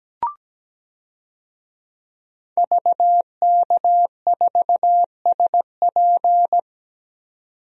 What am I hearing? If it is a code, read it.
Morse code, heard as a keyed tone, about 17 wpm, 700 Hz. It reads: VK4SP
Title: DXCC-100 Call Signs - 5 Characters - 17wpm